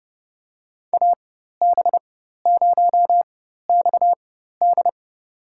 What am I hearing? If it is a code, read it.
Morse code, heard as a keyed tone, about 31 wpm, spaced as for 14 wpm, 700 Hz. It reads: AB0XD
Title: USA Call Signs - 4x Character Spacing - 30wpm